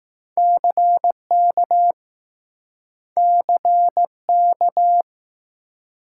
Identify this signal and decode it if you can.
Morse code, heard as a keyed tone, about 20 wpm, 700 Hz. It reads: CK CK